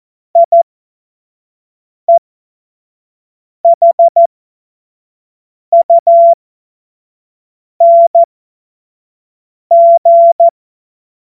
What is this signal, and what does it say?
Morse code, heard as a keyed tone, about 14 wpm, spaced as for 5 wpm, 680 Hz. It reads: IEHUNG
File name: Witz-29-Juli-2025-25-F70